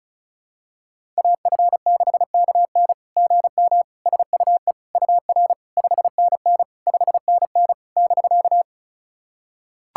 Morse code, heard as a keyed tone, about 36 wpm, 690 Hz. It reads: AF6KN GM SUE UR 5NN 5NN <BK>
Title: SOTA QSO Practice 35wpm